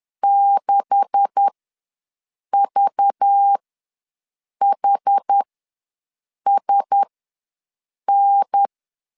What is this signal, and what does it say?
Morse code, heard as a keyed tone, about 10 wpm, spaced as for 6 wpm, 800 Hz. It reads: 6VHSN
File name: LEZIONI / 9 286 ws 10wpm 4032